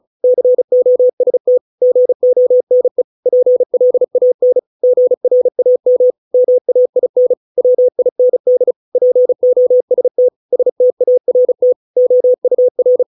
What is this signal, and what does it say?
Morse code, heard as a keyed tone, about 34 wpm, 500 Hz. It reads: COST GONE PLAN GRAM MAIN WIND POST START OUR